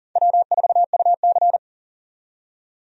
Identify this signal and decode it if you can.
Morse code, heard as a keyed tone, about 40 wpm, 700 Hz. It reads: W4UC
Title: USA Call Signs - 4 Characters - 40wpm